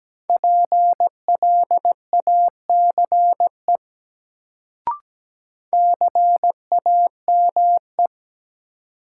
Morse code, heard as a keyed tone, about 17 wpm, 700 Hz. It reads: PLACE CAME
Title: Sentences from Top 200 Words - New Word Drill - 17wpm